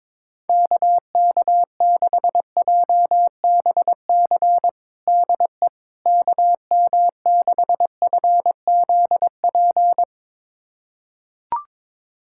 Morse code, heard as a keyed tone, about 22 wpm, 700 Hz. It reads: KK6JBC DE KM6FZP
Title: USA Callsign DE Callsign - 6 Characters - 22wpm